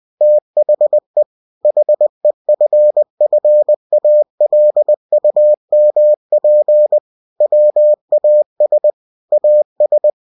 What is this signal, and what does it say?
Morse code, heard as a keyed tone, about 20 wpm, 600 Hz. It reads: THE HEFFALUMP WAS AS
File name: cw-books / winnie-the-pooh / winnie-20wpm-page-0035